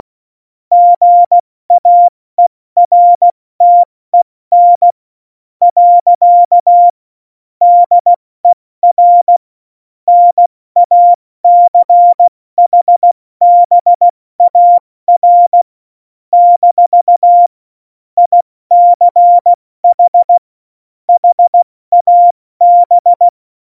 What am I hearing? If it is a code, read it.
Morse code, heard as a keyed tone, about 16 wpm, spaced as for 14 wpm, 700 Hz. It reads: GAERTEN. DER NACHBAR - ICH HAB
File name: Morsekurs / Witze / Witz 07 April 2020, 70 F80